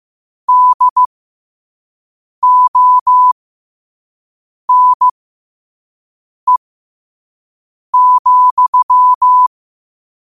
Morse code, heard as a keyed tone, about 15 wpm, spaced as for 5 wpm, 1000 Hz. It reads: DONE,
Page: New quotes every day in morse code at 5 Words per minute.